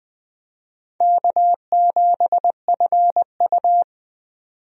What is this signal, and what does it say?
Morse code, heard as a keyed tone, about 20 wpm, 700 Hz. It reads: K7FU
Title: USA Call Signs - 4 Characters - 20wpm